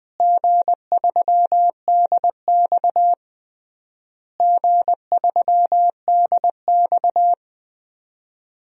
Morse code, heard as a keyed tone, about 20 wpm, 700 Hz. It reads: G3DX G3DX
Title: DXCC-100 Call Signs - 4 Characters - Repeated 3x - 20wpm